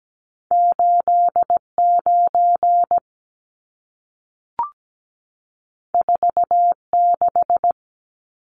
Morse code, heard as a keyed tone, about 17 wpm, 700 Hz. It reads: 89 46
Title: Numbers - 2-Digit 17wpm